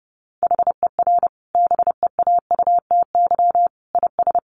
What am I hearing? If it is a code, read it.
Morse code, heard as a keyed tone, about 30 wpm, 700 Hz. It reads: HER BEAUTY IS
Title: Sentences from Top 1100 Words 30wpm